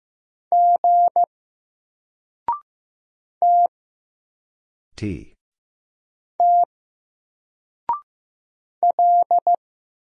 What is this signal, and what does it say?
Morse code, heard as a keyed tone, about 15 wpm, 700 Hz. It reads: G T T L
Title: Single Character (Letter:Number:Punctuation:Prosign) 15wpm